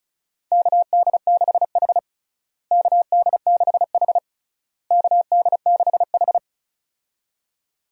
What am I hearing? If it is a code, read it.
Morse code, heard as a keyed tone, about 34 wpm, 700 Hz. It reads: KD6H KD6H KD6H
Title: USA Call Signs - 4 Characters - Repeated 3x - 35wpm